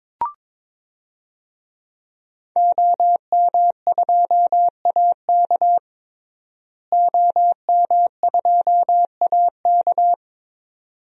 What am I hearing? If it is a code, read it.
Morse code, heard as a keyed tone, about 21 wpm, 700 Hz. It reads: OM2AK OM2AK
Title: DXCC-100 Call Signs - 5 Characters - Repeated 2x - 22wpm